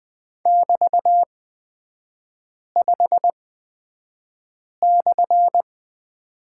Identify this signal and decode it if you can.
Morse code, heard as a keyed tone, about 20 wpm, spaced as for 5 wpm, 700 Hz. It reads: <BT>5/
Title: 224 Lesson 36e - BT - Three Characters 20wpm